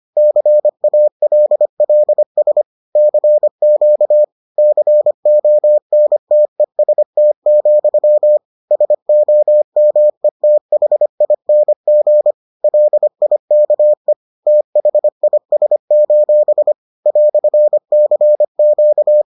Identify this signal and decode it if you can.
Morse code, heard as a keyed tone, about 25 wpm, 600 Hz. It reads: CALLS CQ CONTEST, SOMETHING LIKE THIS: "CQ